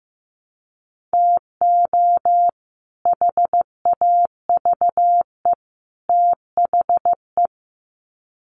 Morse code, heard as a keyed tone, about 15 wpm, 700 Hz. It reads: TO HAVE THE
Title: Sets of 3 Words 15wpm